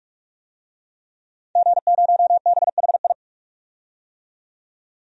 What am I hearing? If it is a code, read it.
Morse code, heard as a keyed tone, about 43 wpm, 700 Hz. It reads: G0BHI
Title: DXCC-100 Call Signs - 5 Characters - 45wpm